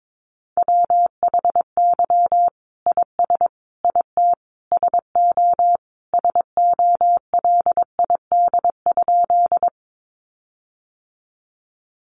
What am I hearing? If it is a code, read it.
Morse code, heard as a keyed tone, about 22 wpm, 700 Hz. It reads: WHY IS IT SO SOLID?
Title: Sentences from Top 1500 Words 22wpm